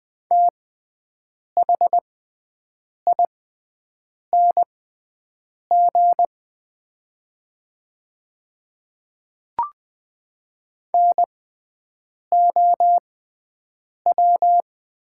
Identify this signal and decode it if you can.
Morse code, heard as a keyed tone, about 20 wpm, spaced as for 7 wpm, 700 Hz. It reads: THING NOW
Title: 6x Character Spacing - Top 100 Words - 20wpm